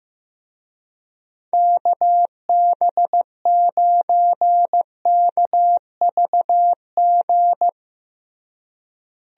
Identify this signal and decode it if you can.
Morse code, heard as a keyed tone, about 15 wpm, 700 Hz. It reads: KB9KVG
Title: USA Call Signs - 6 Characters - Repeated 3x - 15wpm